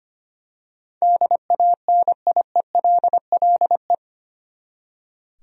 Morse code, heard as a keyed tone, about 25 wpm, 700 Hz. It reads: DANIELLE
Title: USA Common Names - 25wpm